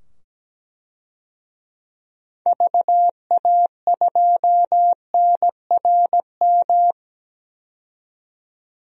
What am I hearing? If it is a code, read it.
Morse code, heard as a keyed tone, about 17 wpm, 700 Hz. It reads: VA2NRM
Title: DXCC-100 Call Signs - 6 Characters - 17wpm